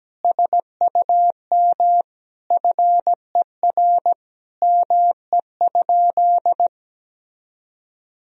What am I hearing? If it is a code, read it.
Morse code, heard as a keyed tone, about 17 wpm, 700 Hz. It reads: SUM FER ME?
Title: Sentences from Top 200 Words and Abbreviations 17wpm